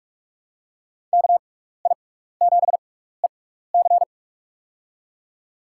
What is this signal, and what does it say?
Morse code, heard as a keyed tone, about 46 wpm, spaced as for 16 wpm, 700 Hz. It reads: KI7EC
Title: USA Call Signs - 6x Character Spacing - 45wpm